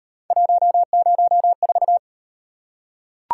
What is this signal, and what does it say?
Morse code, heard as a keyed tone, about 32 wpm, 700 Hz. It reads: 104